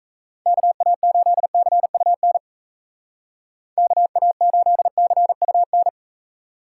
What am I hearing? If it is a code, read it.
Morse code, heard as a keyed tone, about 35 wpm, 700 Hz. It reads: KA8CUN KA8CUN
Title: Speed Racing - USA 2x3 Call Signs 28wpm